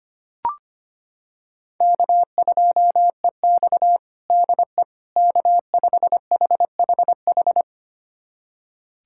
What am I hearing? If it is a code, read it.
Morse code, heard as a keyed tone, about 25 wpm, 700 Hz. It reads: K2EX DE K5HHH
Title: USA Callsign DE Callsign 25wpm